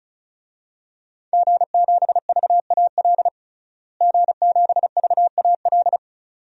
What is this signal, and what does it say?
Morse code, heard as a keyed tone, about 34 wpm, 700 Hz. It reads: G7VAL G7VAL
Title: DXCC-100 Call Signs - 5 Characters - Repeated 2x - 35wpm